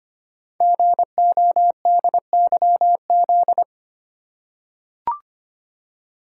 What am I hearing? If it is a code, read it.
Morse code, heard as a keyed tone, about 25 wpm, 700 Hz. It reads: GODYZ
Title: Code Groups - 5 Letters - 25wpm